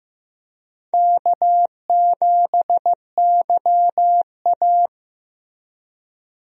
Morse code, heard as a keyed tone, about 15 wpm, 700 Hz. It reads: K7YA
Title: USA Call Signs - 4 Characters - Repeated 2x - 15wpm